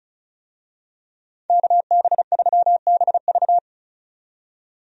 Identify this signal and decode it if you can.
Morse code, heard as a keyed tone, about 36 wpm, 700 Hz. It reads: KB3BV
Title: USA Call Signs - 5 Characters - 35wpm